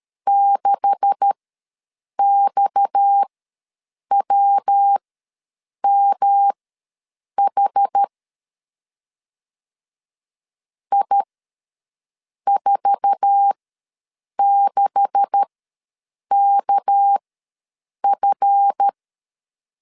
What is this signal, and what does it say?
Morse code, heard as a keyed tone, about 13 wpm, spaced as for 7 wpm, 800 Hz. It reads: 6XWMH I46KF
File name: LEZIONI / 9 286 ws 12wpm 3351